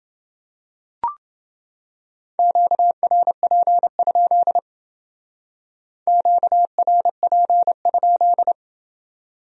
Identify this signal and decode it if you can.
Morse code, heard as a keyed tone, about 28 wpm, 700 Hz. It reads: QRP? QRP?